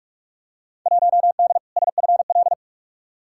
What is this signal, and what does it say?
Morse code, heard as a keyed tone, about 46 wpm, 700 Hz. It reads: 1D SFL
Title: Field Day Exchanges - 45wpm